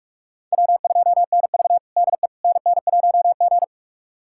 Morse code, heard as a keyed tone, about 41 wpm, 700 Hz. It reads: W2NV DE NN1G
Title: USA Callsign DE Callsign - 4 Characters - 45wpm